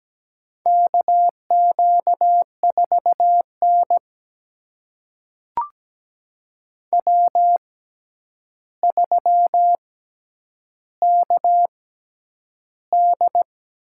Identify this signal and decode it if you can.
Morse code, heard as a keyed tone, about 17 wpm, spaced as for 9 wpm, 700 Hz. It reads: KQ4N W3KD
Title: USA Call Signs - 4 Characters - 6x Character Spacing - 17wpm